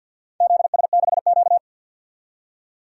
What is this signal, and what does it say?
Morse code, heard as a keyed tone, about 47 wpm, 700 Hz. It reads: ZS6Q
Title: DXCC-100 Call Signs - 4 Characters - 50wpm